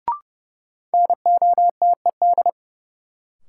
Morse code, heard as a keyed tone, about 30 wpm, 700 Hz. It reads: NOTED
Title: Sets of 5 Words - New Word Drill - 30wpm